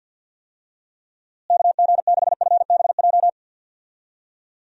Morse code, heard as a keyed tone, about 52 wpm, 700 Hz. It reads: KG6FBJ
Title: USA Call Signs - 6 Characters - Repeated 2x - 50wpm